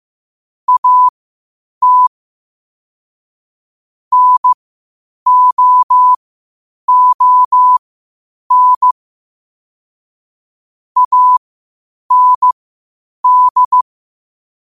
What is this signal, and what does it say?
Morse code, heard as a keyed tone, about 15 wpm, spaced as for 8 wpm, 1000 Hz. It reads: AT NOON AND
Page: qotd-at-8-wpm